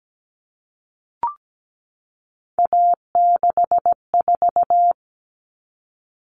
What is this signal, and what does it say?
Morse code, heard as a keyed tone, about 17 wpm, 700 Hz. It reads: A64
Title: Common Cut Numbers - 3-Digit 17wpm